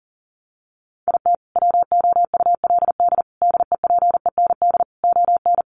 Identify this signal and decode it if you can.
Morse code, heard as a keyed tone, about 40 wpm, 700 Hz. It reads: IT WOULD DEPEND ON
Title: Sentences from Top 1300 Words 40wpm